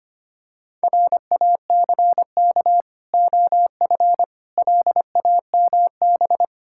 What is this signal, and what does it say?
Morse code, heard as a keyed tone, about 25 wpm, 700 Hz. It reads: RACK OF LAMB
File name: practice / 25wpm / Difficult Words with Hints - 25wpm